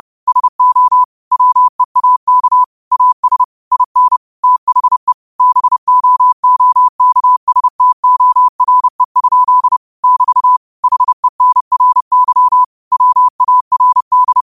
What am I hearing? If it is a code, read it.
Morse code, heard as a keyed tone, about 30 wpm, 1000 Hz. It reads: SO WEAK AS IN THE BOOKSTORE? <BT> HENRY WARD
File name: qotdfiles / 30wpm